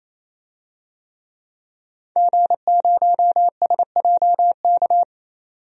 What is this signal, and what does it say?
Morse code, heard as a keyed tone, about 28 wpm, 700 Hz. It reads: G0SJK
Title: DXCC-100 Call Signs - 5 Characters - 28wpm